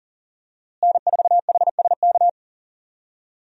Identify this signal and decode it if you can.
Morse code, heard as a keyed tone, about 40 wpm, 700 Hz. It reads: N4HSK
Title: USA Call Signs - 5 Characters - Repeated 3x - 40wpm